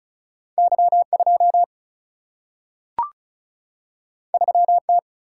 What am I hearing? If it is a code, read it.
Morse code, heard as a keyed tone, about 36 wpm, 700 Hz. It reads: Y2 3T
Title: Single-Letter Single-Number 35wpm